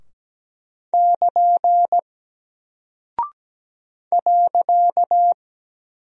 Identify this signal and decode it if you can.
Morse code, heard as a keyed tone, about 17 wpm, 700 Hz. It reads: <KN> .
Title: Single Character (Letter:Number:Punctuation:Prosign) 17wpm